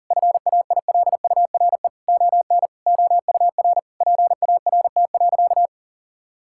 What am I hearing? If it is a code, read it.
Morse code, heard as a keyed tone, about 37 wpm, 700 Hz. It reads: FAILURE ON OUR PART.
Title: Sentences from Top 1000 Words - Review Part 6 - 40wpm